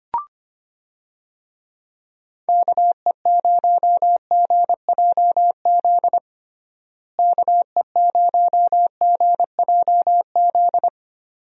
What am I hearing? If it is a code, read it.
Morse code, heard as a keyed tone, about 25 wpm, 700 Hz. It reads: KE0GJZ KE0GJZ
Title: USA Call Signs - 6 Characters - Repeated 2x - 25wpm